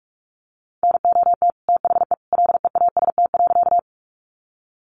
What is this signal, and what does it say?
Morse code, heard as a keyed tone, about 44 wpm, 700 Hz. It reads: NOT THE LEAST.
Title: Sentences from Top 700 Words 45wpm